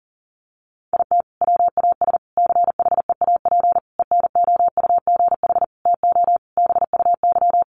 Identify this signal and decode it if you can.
Morse code, heard as a keyed tone, about 40 wpm, 700 Hz. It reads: IT WAS CHEAP ENOUGH TO BUY
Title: Sentences from Top 1200 Words 40wpm